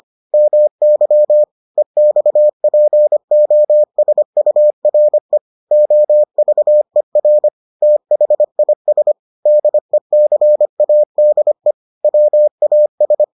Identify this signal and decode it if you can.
Morse code, heard as a keyed tone, about 25 wpm, 600 Hz. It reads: MY EXPOSURE OVER THIS DECADE WAS